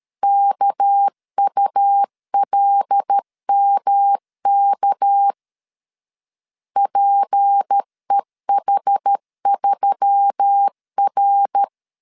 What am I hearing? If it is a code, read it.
Morse code, heard as a keyed tone, about 13 wpm, 800 Hz. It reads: KULMK PEH3R
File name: LEZIONI / 6 PKX0 / 6 PK 12wpm 2006